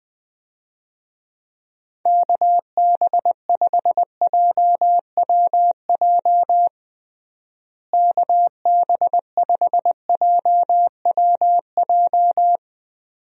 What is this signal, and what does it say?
Morse code, heard as a keyed tone, about 20 wpm, 700 Hz. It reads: KB5JWJ KB5JWJ
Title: USA Call Signs - 6 Characters - Repeated 2x - 20wpm